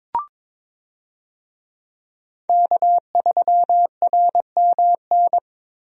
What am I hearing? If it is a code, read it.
Morse code, heard as a keyed tone, about 22 wpm, 700 Hz. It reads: K3RMN
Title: USA Call Signs - 5 Characters - Repeated 3x - 22wpm